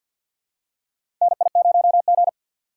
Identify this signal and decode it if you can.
Morse code, heard as a keyed tone, about 52 wpm, 700 Hz. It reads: NI0G
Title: USA Call Signs - 4 Characters - 3x Character Spacing - 50wpm